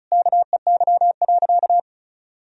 Morse code, heard as a keyed tone, about 36 wpm, 700 Hz. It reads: KEY.
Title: Sentences from Top 1000 Words - Review Part 4 - 35wpm